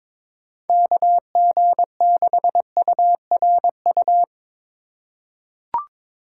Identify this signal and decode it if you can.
Morse code, heard as a keyed tone, about 22 wpm, 700 Hz. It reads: KG6URU
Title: USA Call Signs - 6 Characters - 2x Character Spacing - 22wpm